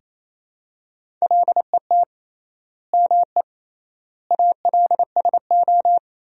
Morse code, heard as a keyed tone, about 28 wpm, 700 Hz. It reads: LET ME ALSO
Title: 3x Word-Spacing - Sentences from Top 400 Words - 28wpm